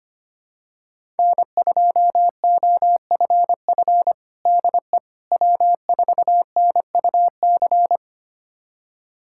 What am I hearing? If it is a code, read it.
Morse code, heard as a keyed tone, about 25 wpm, 700 Hz. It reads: N2OFF DE W4NUC
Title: USA Callsign DE Callsign - 5 Characters - 25wpm